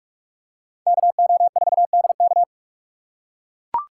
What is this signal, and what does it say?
Morse code, heard as a keyed tone, about 44 wpm, 700 Hz. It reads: KO4DK
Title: USA Call Signs - 5 Characters - 6x Character Spacing - 45wpm